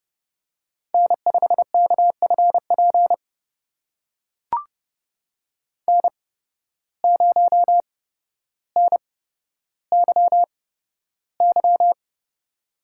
Morse code, heard as a keyed tone, about 30 wpm, spaced as for 13 wpm, 700 Hz. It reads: N5KFP N0NYY
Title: USA Call Signs - 5 Characters - 8x Character Spacing - 30wpm